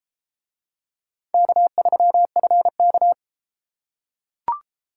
Morse code, heard as a keyed tone, about 31 wpm, 700 Hz. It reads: K3FK